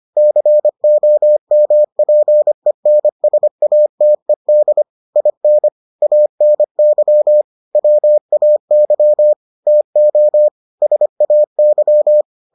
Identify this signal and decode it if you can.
Morse code, heard as a keyed tone, about 25 wpm, 600 Hz. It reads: COMPENSATED IN ANY WAY TO SAY